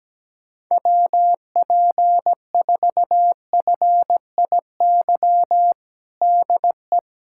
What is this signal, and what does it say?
Morse code, heard as a keyed tone, about 17 wpm, 700 Hz. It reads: WP4FIY DE
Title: USA Callsign DE Callsign 17wpm